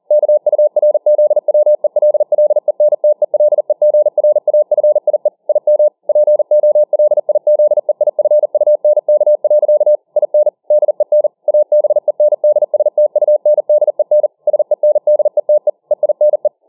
Morse code, heard as a keyed tone, about 40 wpm, 600 Hz. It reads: KURZWELLENTELEGRAFIE IM POLIZEIFUNK. IN DEN ABENDSTUNDEN SENDETE EINE